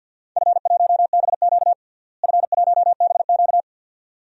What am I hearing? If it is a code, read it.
Morse code, heard as a keyed tone, about 49 wpm, 700 Hz. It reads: F1BQ F1BQ
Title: DXCC-100 Call Signs - 4 Characters - Repeated 2x - 50wpm